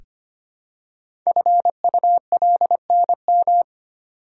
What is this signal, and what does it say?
Morse code, heard as a keyed tone, about 25 wpm, 700 Hz. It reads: FULNM
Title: Code Groups - 5 Letters - 25wpm